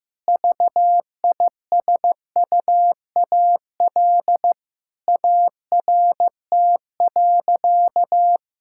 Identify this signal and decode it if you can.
Morse code, heard as a keyed tone, about 15 wpm, 700 Hz. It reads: VISUAL ART.
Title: Sentences from Top 1000 Words - Review Part 6 - 15wpm